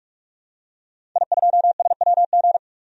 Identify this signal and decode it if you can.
Morse code, heard as a keyed tone, about 46 wpm, 700 Hz. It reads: I2SWG
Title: DXCC-100 Call Signs - 5 Characters - 45wpm